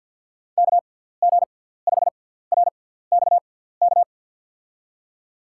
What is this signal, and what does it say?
Morse code, heard as a keyed tone, about 51 wpm, spaced as for 17 wpm, 700 Hz. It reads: KG5RXK
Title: USA Call Signs - 6 Characters - 6x Character Spacing - 50wpm